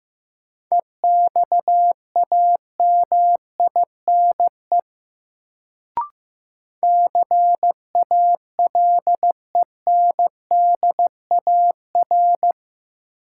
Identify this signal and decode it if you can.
Morse code, heard as a keyed tone, about 15 wpm, 700 Hz. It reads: EXAMINE CALENDAR
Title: Sentences from Top 1600 Words - New Word Drill Alt - 15wpm